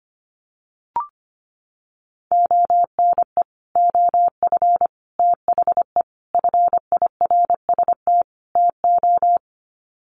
Morse code, heard as a keyed tone, about 25 wpm, 700 Hz. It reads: ONE OF THE FIRST TO
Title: Sets of 5 Words Encore 25wpm